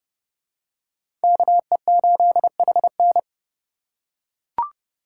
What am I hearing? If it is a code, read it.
Morse code, heard as a keyed tone, about 30 wpm, 700 Hz. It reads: KE8HN
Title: USA Call Signs - 5 Characters - 4x Character Spacing - 30wpm